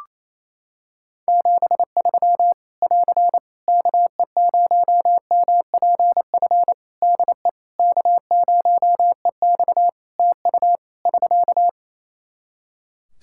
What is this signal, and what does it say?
Morse code, heard as a keyed tone, about 27 wpm, 700 Hz. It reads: 73 <AR> KE0MPF DE K0EX TU <SK>
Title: Practice QSO - Exchange 4 - Ending - 28wpm